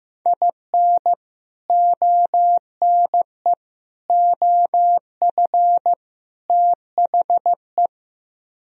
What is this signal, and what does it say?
Morse code, heard as a keyed tone, about 16 wpm, 700 Hz. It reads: IN ONE OF THE
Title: Speed-Racing - Sets of 4 Words 15wpm